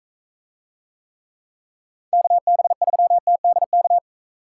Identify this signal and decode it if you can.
Morse code, heard as a keyed tone, about 45 wpm, 700 Hz. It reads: KB3TDK